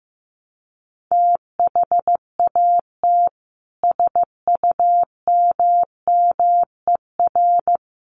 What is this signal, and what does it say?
Morse code, heard as a keyed tone, about 15 wpm, 700 Hz. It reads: THAT SUMMER
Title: Sentences from Top 1100 Words 15wpm